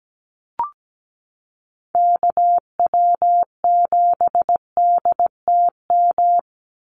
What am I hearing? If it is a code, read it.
Morse code, heard as a keyed tone, about 17 wpm, 700 Hz. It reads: KW7DTM
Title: USA Call Signs 17wpm